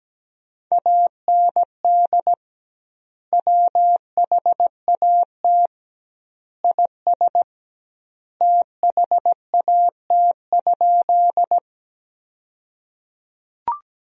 Morse code, heard as a keyed tone, about 17 wpm, 700 Hz. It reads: AND WHAT IS THAT?
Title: 2x Word-Spacing - Sentences from Top 100 Words - 17wpm